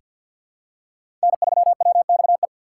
Morse code, heard as a keyed tone, about 47 wpm, 700 Hz. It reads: N3WXE
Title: USA Call Signs - 5 Characters - 3x Character Spacing - 50wpm